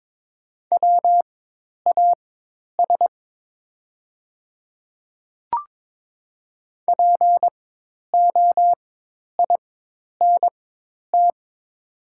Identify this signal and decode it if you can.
Morse code, heard as a keyed tone, about 22 wpm, spaced as for 10 wpm, 700 Hz. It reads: WAS POINT
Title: 4x Character Spacing - Top 200 Words - 22wpm